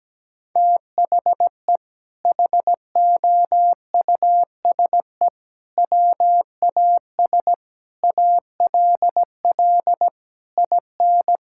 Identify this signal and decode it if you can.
Morse code, heard as a keyed tone, about 17 wpm, 700 Hz. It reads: THE HOUSE WAS ALL IN
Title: Sentences from Top 1600 Words 17wpm